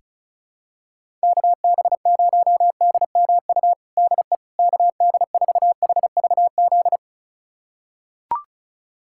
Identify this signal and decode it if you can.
Morse code, heard as a keyed tone, about 36 wpm, 700 Hz. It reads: KB0DMU DE KD4HVZ
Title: USA Callsign DE Callsign 35wpm